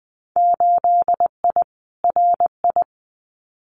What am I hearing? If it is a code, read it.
Morse code, heard as a keyed tone, about 20 wpm, 700 Hz. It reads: 8I RI
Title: Winter Field Day Exchanges Encore 20wpm